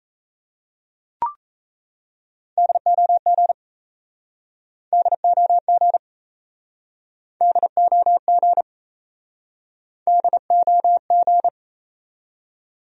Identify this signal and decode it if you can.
Morse code, heard as a keyed tone, about 37 wpm, 700 Hz. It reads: DOG DOG DOG DOG